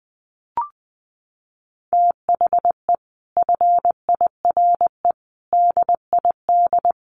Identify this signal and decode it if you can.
Morse code, heard as a keyed tone, about 20 wpm, 700 Hz. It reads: THE FIRE DID
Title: Sentences from Top 1300 Words 20wpm